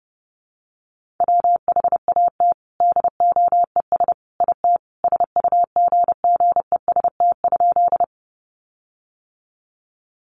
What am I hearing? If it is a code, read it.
Morse code, heard as a keyed tone, about 29 wpm, 700 Hz. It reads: WHAT DOES IT SUGGEST?
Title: Sentences from Top 1300 Words 30wpm